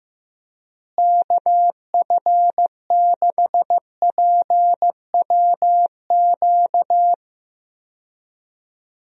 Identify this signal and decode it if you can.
Morse code, heard as a keyed tone, about 15 wpm, 700 Hz. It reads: KF6PWQ
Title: USA Call Signs - 6 Characters - Repeated 3x - 15wpm